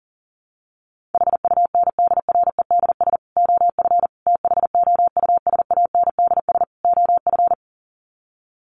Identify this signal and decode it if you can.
Morse code, heard as a keyed tone, about 40 wpm, 700 Hz. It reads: HUNDREDS OF THOUSANDS OF
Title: Sets of 4 Words 40wpm